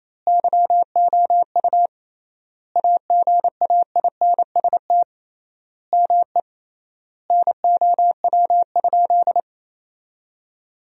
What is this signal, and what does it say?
Morse code, heard as a keyed tone, about 28 wpm, 700 Hz. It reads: YOU AGAINST ME NOW?
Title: 3x Word-Spacing - Sentences from Top 400 Words - 28wpm